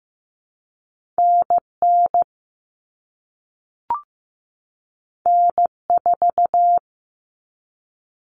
Morse code, heard as a keyed tone, about 15 wpm, 700 Hz. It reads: NN N4
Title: All Cut Numbers - 2-Digit 15wpm